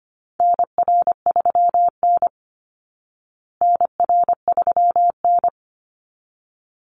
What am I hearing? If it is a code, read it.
Morse code, heard as a keyed tone, about 25 wpm, 700 Hz. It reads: NR3N NR3N
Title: Call Signs Repeated 3x - 25wpm